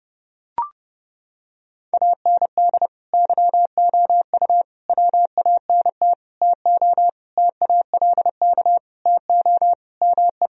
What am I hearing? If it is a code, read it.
Morse code, heard as a keyed tone, about 32 wpm, 700 Hz. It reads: AND YOU WANT TO TALK TO ME
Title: Sentences from Top 1000 Words - Review Part 4 - 30wpm